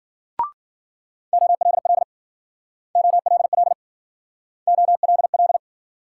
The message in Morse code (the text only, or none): OLL OLL OLL